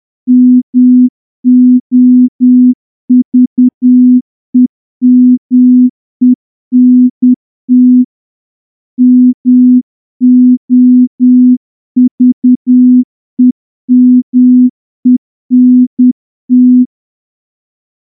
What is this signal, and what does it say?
Morse code, heard as a keyed tone, about 10 wpm, 250 Hz. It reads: MOVEMENT MOVEMENT
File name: morse-training-10WPM